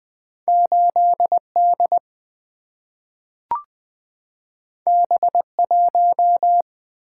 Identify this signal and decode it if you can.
Morse code, heard as a keyed tone, about 20 wpm, 700 Hz. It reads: 8D B1
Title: Two Letter-Number Combinations - 20wpm